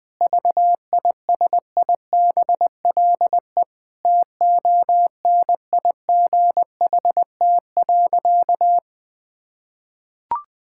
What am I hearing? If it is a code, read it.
Morse code, heard as a keyed tone, about 21 wpm, 700 Hz. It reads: VISIBLE TONIGHT.
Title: Sentences with 2-Word Phrases - 20wpm